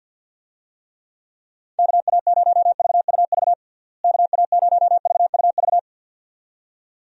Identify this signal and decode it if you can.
Morse code, heard as a keyed tone, about 46 wpm, 700 Hz. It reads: KA0VUV KA0VUV
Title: USA Call Signs - 6 Characters - Repeated 2x - 50wpm